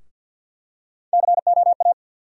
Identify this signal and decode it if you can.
Morse code, heard as a keyed tone, about 52 wpm, 700 Hz. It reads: COA
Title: USA Call Sign Postfixes - 3 Letters 50wpm